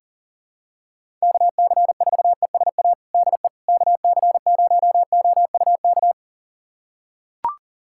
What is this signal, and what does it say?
Morse code, heard as a keyed tone, about 40 wpm, 690 Hz. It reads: KC4ESA DE KC0OUK
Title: USA Callsign DE Callsign - 6 Characters - 40wpm